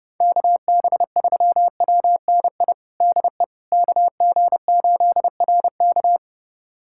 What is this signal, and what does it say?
Morse code, heard as a keyed tone, about 32 wpm, 700 Hz. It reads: KB3WNI DE KG8RK
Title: USA Callsign DE Callsign 30wpm